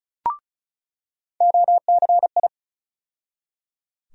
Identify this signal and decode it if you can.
Morse code, heard as a keyed tone, about 36 wpm, 700 Hz. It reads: OCI
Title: USA Call Sign Postfixes - 3 Letters 35wpm